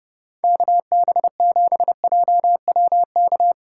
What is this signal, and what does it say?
Morse code, heard as a keyed tone, about 29 wpm, 700 Hz. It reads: KB7JWK